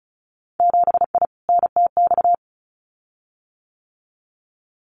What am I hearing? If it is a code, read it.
Morse code, heard as a keyed tone, about 36 wpm, 700 Hz. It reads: 7I NTX
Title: Winter Field Day Exchanges Encore 35wpm